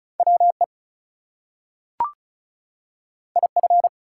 Morse code, heard as a keyed tone, about 36 wpm, 700 Hz. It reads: WE IF
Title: Two Letter Words - 35wpm